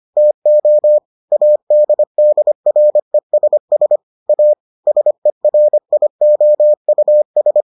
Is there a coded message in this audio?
Yes: Morse code from 0.2 to 7.6 s, heard as a keyed tone, about 25 wpm, 600 Hz. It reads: TO ADDRESS A SERIOUS